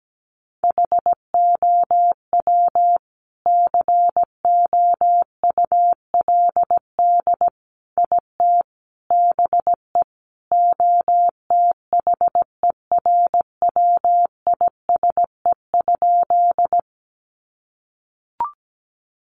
Morse code, heard as a keyed tone, about 17 wpm, 700 Hz. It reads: HOW COULD IT BE OTHERWISE?
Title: Sentences from Top 1200 Words 17wpm